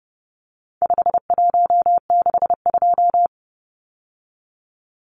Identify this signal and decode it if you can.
Morse code, heard as a keyed tone, about 30 wpm, 700 Hz. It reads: 5162
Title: Numbers - 4-Digit 30wpm